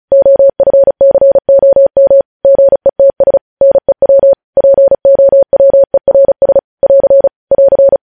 Morse code, heard as a keyed tone, about 36 wpm, 550 Hz. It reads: OFCOM GETS NEW POWERS <AR> <AR>
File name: podcast / News headlines-1x-35x35